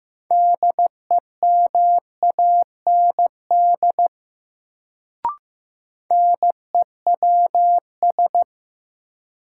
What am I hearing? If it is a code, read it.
Morse code, heard as a keyed tone, about 16 wpm, 700 Hz. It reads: DEMAND NEWS
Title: Sentences from Top 800 Words - New Word Drill - 15wpm